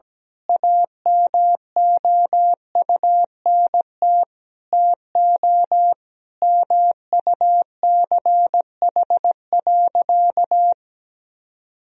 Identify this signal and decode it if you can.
Morse code, heard as a keyed tone, about 17 wpm, 700 Hz. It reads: AMOUNT TO MUCH.
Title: Sentences from Top 1000 Words - Review Part 4 - 17wpm